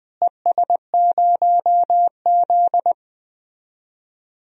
Morse code, heard as a keyed tone, about 20 wpm, 700 Hz. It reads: ES0Z
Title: DXCC-100 Call Signs - 4 Characters - Repeated 2x - 20wpm